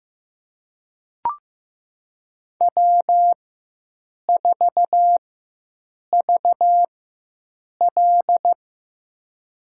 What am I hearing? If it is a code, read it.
Morse code, heard as a keyed tone, about 15 wpm, spaced as for 7 wpm, 700 Hz. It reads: W4VL
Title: USA Call Signs - 4 Characters - 4x Character Spacing - 15wpm